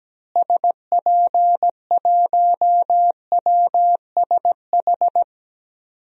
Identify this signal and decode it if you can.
Morse code, heard as a keyed tone, about 17 wpm, 700 Hz. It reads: SP1WSH
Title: DXCC-100 Call Signs - 6 Characters - Repeated 3x - 17wpm